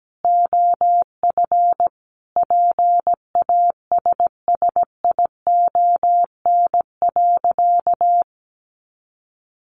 Morse code, heard as a keyed tone, about 17 wpm, 700 Hz. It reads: OF PASSION.
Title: Sentences from Top 1300 Words 17wpm